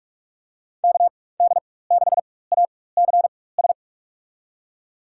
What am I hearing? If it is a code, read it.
Morse code, heard as a keyed tone, about 46 wpm, spaced as for 21 wpm, 700 Hz. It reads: KD6ACS
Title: USA Call Signs - 4x Character Spacing - 45wpm